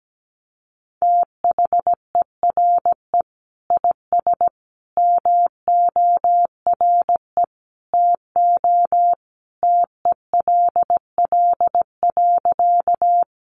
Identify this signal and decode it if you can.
Morse code, heard as a keyed tone, about 17 wpm, 700 Hz. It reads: THERE IS MORE TO TELL.
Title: Sentences from Top 200 Words 17wpm